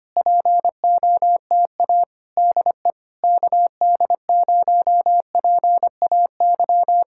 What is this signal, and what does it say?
Morse code, heard as a keyed tone, about 24 wpm, 700 Hz. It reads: POTA DE KD0PAY
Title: Speed-Racing Common QSO Phrases - 25wpm